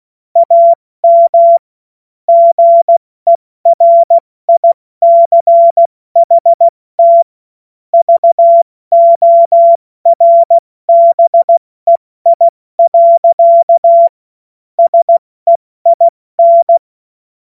Morse code, heard as a keyed tone, about 16 wpm, 680 Hz. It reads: AM GERICHT VORBEI. SEIN
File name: Morsekurs / Witze / Witz 31. Mai 2022 70 F80